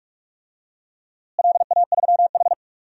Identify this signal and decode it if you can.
Morse code, heard as a keyed tone, about 42 wpm, 700 Hz. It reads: LA3H
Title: DXCC-100 Call Signs - 4 Characters - Repeated 3x - 45wpm